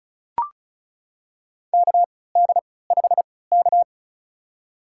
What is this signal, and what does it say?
Morse code, heard as a keyed tone, about 36 wpm, spaced as for 20 wpm, 700 Hz. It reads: KD5K
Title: USA Call Signs - 4 Characters - 3x Character Spacing - 35wpm